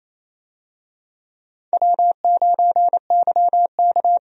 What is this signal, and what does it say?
Morse code, heard as a keyed tone, about 28 wpm, 700 Hz. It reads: W9YK
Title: USA Call Signs - 4 Characters - Repeated 3x - 28wpm